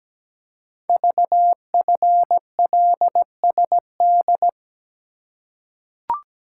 Morse code, heard as a keyed tone, about 17 wpm, 700 Hz. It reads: VFLSD
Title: Code Groups - 5 Letters - 17wpm